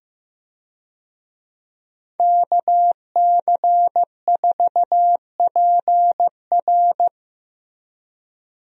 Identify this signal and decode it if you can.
Morse code, heard as a keyed tone, about 15 wpm, 700 Hz. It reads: KC4PR
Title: USA Call Signs - 5 Characters - Repeated 2x - 15wpm